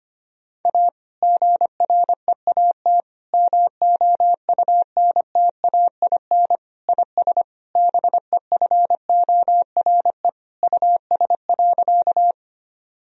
Morse code, heard as a keyed tone, about 25 wpm, 700 Hz. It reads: A GREAT MOUNTAIN IS BEFORE US.
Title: Sentences from Top 1000 Words - Review Part 1 - 25wpm